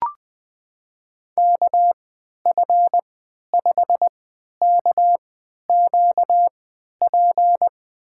Morse code, heard as a keyed tone, about 20 wpm, spaced as for 11 wpm, 700 Hz. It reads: KF5KQP
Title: USA Call Signs - 6 Characters - 3x Character Spacing - 20wpm